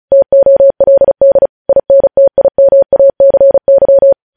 Morse code, heard as a keyed tone, about 36 wpm, 550 Hz. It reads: TOLD INTIMACY
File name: podcast / News headlines-1x-35x35